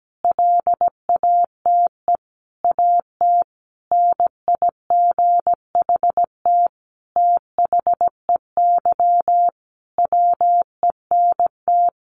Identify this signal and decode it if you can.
Morse code, heard as a keyed tone, about 17 wpm, 700 Hz. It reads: LATE AT NIGHT THEY WENT
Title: Sentences from Top 600 Words 17wpm